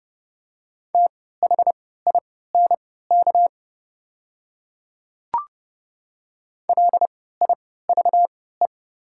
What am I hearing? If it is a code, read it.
Morse code, heard as a keyed tone, about 30 wpm, spaced as for 17 wpm, 700 Hz. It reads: THINK LIVE
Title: 3x Character Spacing - Top 200 Words - 30wpm